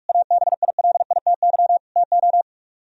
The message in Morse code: ABILITY TO

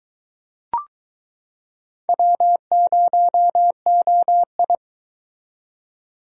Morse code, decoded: W0OI

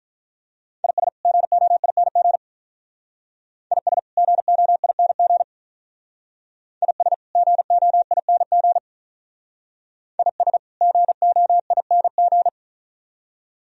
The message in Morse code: IS GOING IS GOING IS GOING IS GOING

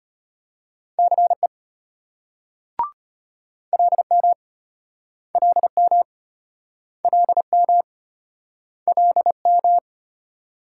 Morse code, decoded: CE LM LM LM LM

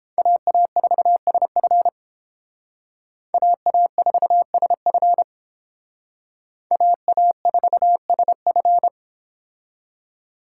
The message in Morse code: AA4SF AA4SF AA4SF

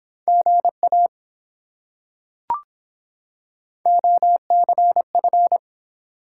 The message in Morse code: GA OCF

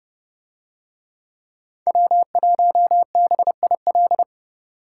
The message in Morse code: W1BIL

